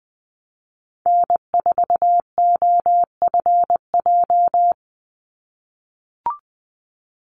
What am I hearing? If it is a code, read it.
Morse code, heard as a keyed tone, about 20 wpm, 700 Hz. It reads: N4OFJ
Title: USA Call Signs 20wpm